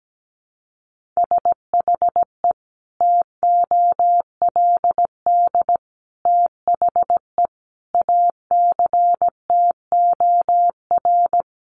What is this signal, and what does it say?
Morse code, heard as a keyed tone, about 17 wpm, 700 Hz. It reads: SHE TOLD THE ACTOR
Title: Sentences from Top 1500 Words 17wpm